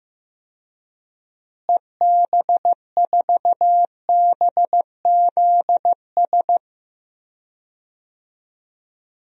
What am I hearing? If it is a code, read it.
Morse code, heard as a keyed tone, about 15 wpm, 700 Hz. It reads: EB4BZS
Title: DXCC-100 Call Signs - 6 Characters - Repeated 3x - 15wpm